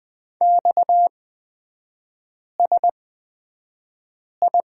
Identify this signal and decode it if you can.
Morse code, heard as a keyed tone, about 20 wpm, spaced as for 5 wpm, 700 Hz. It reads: XSI
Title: 229 Lesson 37e - Letter x - Three Characters 20wpm